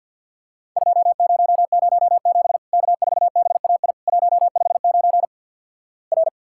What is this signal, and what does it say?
Morse code, heard as a keyed tone, about 50 wpm, 700 Hz. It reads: 2007 K4BAI 159 R
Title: Worked All Europe DX Contest - QTC Exchange - 50wpm